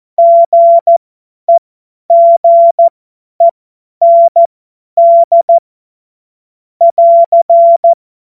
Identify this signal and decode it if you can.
Morse code, heard as a keyed tone, about 14 wpm, spaced as for 10 wpm, 680 Hz. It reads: GEGEND <AR>